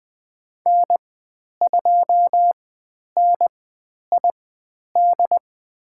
Morse code, heard as a keyed tone, about 20 wpm, spaced as for 10 wpm, 700 Hz. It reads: N2NID
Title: 109 Lesson 18g - Letter m - Call Signs 20wpm